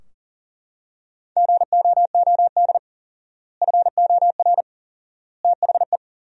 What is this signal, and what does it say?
Morse code, heard as a keyed tone, about 40 wpm, 700 Hz. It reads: GOOD FOR THE
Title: 4x Word-Spacing - Sentences from Top 300 Words - 40wpm